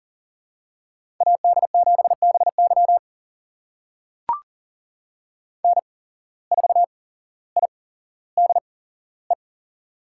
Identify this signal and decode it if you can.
Morse code, heard as a keyed tone, about 40 wpm, spaced as for 17 wpm, 700 Hz. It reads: AD7BY N4IDE
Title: USA Call Signs - 5 Characters - 8x Character Spacing - 40wpm